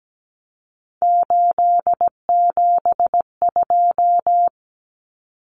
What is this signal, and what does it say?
Morse code, heard as a keyed tone, about 17 wpm, 700 Hz. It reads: 872
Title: Common Cut Numbers - 3-Digit 17wpm